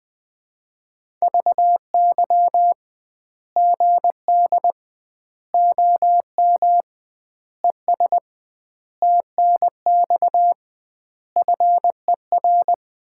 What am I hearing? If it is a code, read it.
Morse code, heard as a keyed tone, about 21 wpm, 700 Hz. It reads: VY GD OM ES TNX FER
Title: Common QSO Phrases - 2x Word-Spacing - 20wpm